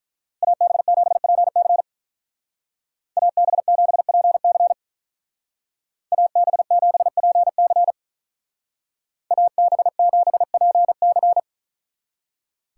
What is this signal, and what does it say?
Morse code, heard as a keyed tone, about 43 wpm, 700 Hz. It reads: AB7PC AB7PC AB7PC AB7PC
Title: Speed Racing - USA 2x2 Call Signs 35wpm